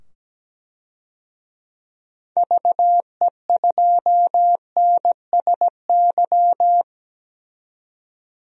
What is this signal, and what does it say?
Morse code, heard as a keyed tone, about 17 wpm, 700 Hz. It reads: VE2NSY
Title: DXCC-100 Call Signs - 6 Characters - 17wpm